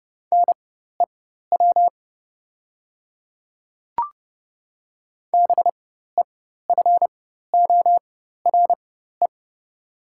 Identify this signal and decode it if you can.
Morse code, heard as a keyed tone, about 30 wpm, spaced as for 14 wpm, 700 Hz. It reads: NEW BEFORE
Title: 4x Character Spacing - Top 200 Words - 30wpm